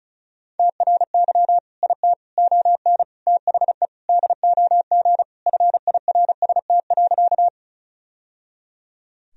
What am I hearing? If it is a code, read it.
Morse code, heard as a keyed tone, about 34 wpm, 700 Hz. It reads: TRY IT ON THE DOG FIRST.
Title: Sentences from Top 1000 Words - Review Part 2 - 35wpm